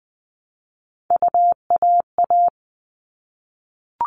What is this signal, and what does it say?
Morse code, heard as a keyed tone, about 20 wpm, 700 Hz. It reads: UAA
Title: All Cut Numbers - 3-Digit 20wpm